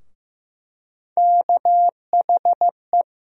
Morse code, heard as a keyed tone, about 16 wpm, 700 Hz. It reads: KHE
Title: USA Call Sign Postfixes - 3 Letters 15wpm